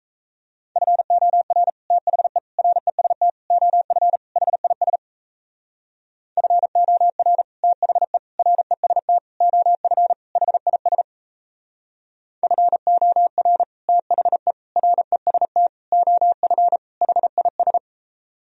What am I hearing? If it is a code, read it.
Morse code, heard as a keyed tone, about 36 wpm, 700 Hz. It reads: FOR THE REST OF HIS FOR THE REST OF HIS FOR THE REST OF HIS